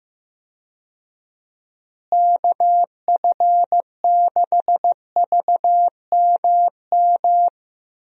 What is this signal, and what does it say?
Morse code, heard as a keyed tone, about 15 wpm, 700 Hz. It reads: KF6VMM
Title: USA Call Signs - 6 Characters - Repeated 3x - 15wpm